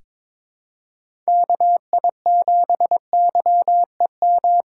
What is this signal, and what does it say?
Morse code, heard as a keyed tone, about 21 wpm, 700 Hz. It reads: KI7YEM
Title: USA Call Signs - 6 Characters - 3x Character Spacing - 22wpm